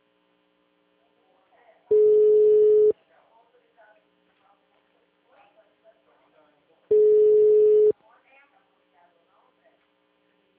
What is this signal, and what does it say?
Ringback tone (Europe, 425 Hz), with speech.